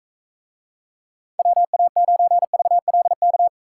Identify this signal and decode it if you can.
Morse code, heard as a keyed tone, about 44 wpm, 700 Hz. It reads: WA9VLK